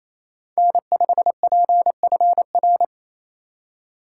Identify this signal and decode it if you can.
Morse code, heard as a keyed tone, about 28 wpm, 700 Hz. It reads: N5PFR
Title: USA Call Signs - 5 Characters - 2x Character Spacing - 28wpm